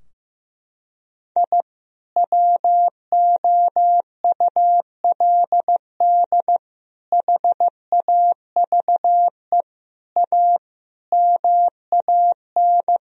Morse code, heard as a keyed tone, about 15 wpm, 700 Hz. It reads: I WOULD HAVE A MAN
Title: Sentences from Top 1000 Words - Review Part 2 - 15wpm